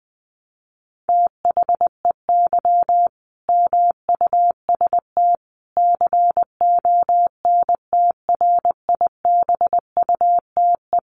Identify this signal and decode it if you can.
Morse code, heard as a keyed tone, about 20 wpm, 700 Hz. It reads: THEY MUST CONTRIBUTE
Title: Sentences from Top 1400 Words 20wpm